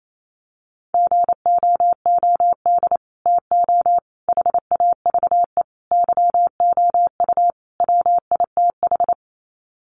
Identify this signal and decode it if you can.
Morse code, heard as a keyed tone, about 28 wpm, 700 Hz. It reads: GOOD TO HAVE YOU WITH
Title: Sets of 5 Words Encore2 28wpm